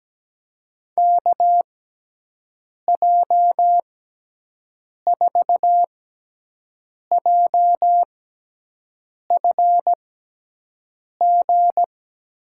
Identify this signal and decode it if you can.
Morse code, heard as a keyed tone, about 17 wpm, spaced as for 6 wpm, 700 Hz. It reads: KJ4JFG
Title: USA Call Signs - 6x Character Spacing - 17wpm